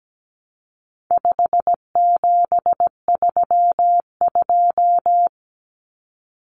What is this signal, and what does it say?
Morse code, heard as a keyed tone, about 17 wpm, 700 Hz. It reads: 5732
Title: Numbers - 4-Digit 17wpm